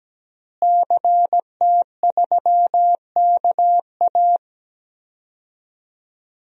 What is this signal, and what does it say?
Morse code, heard as a keyed tone, about 17 wpm, 700 Hz. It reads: CT3KA
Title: DXCC-100 Call Signs - 5 Characters - Repeated 3x - 17wpm